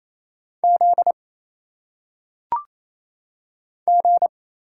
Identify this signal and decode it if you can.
Morse code, heard as a keyed tone, about 28 wpm, 700 Hz. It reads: Z G
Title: Single Letters - Phonetic Pronunciation 28wpm